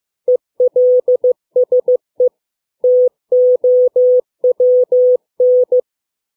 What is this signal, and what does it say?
Morse code, heard as a keyed tone, about 15 wpm, 500 Hz. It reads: ELSE TOWN